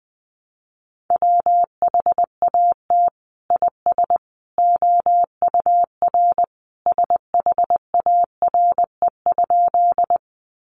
Morse code, heard as a keyed tone, about 20 wpm, 700 Hz. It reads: WHAT IS OUR SHARE?
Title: Sentences from Top 900 Words 20wpm